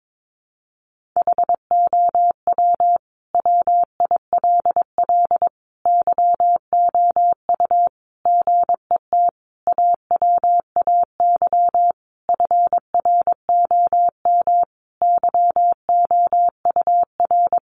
Morse code, heard as a keyed tone, about 22 wpm, 700 Hz. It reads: HOW WILL YOU GET AWAY FROM YOUR
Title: Sentences from Top 1500 Words 22wpm